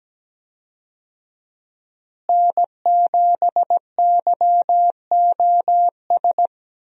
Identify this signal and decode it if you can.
Morse code, heard as a keyed tone, about 17 wpm, 700 Hz. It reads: N7YOS
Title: USA Call Signs - 5 Characters - Repeated 2x - 17wpm